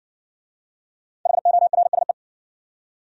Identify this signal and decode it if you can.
Morse code, heard as a keyed tone, about 60 wpm, 700 Hz. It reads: HORSE